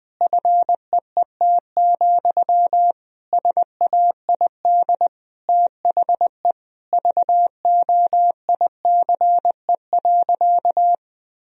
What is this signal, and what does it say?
Morse code, heard as a keyed tone, about 20 wpm, 700 Hz. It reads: FEET, SAID THE VOICE.
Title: Sentences from Top 1000 Words - Review Part 1 - 20wpm